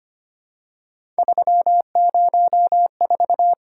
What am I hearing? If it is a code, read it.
Morse code, heard as a keyed tone, about 25 wpm, 700 Hz. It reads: 304
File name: Three Letter-Number Combinations - 25wpm